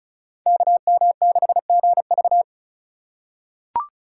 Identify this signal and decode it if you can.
Morse code, heard as a keyed tone, about 34 wpm, 700 Hz. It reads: KM6GV
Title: USA Call Signs - 5 Characters - 3x Character Spacing - 35wpm